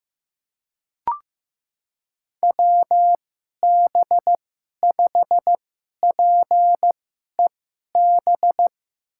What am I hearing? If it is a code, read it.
Morse code, heard as a keyed tone, about 15 wpm, spaced as for 11 wpm, 700 Hz. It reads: WB5PEB
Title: USA Call Signs - 6 Characters - 2x Character Spacing - 15wpm